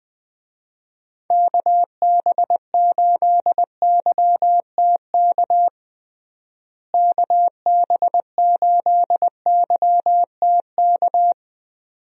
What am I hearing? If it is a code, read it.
Morse code, heard as a keyed tone, about 21 wpm, 700 Hz. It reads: KB8YTK KB8YTK
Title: USA Call Signs - 6 Characters - Repeated 2x - 20wpm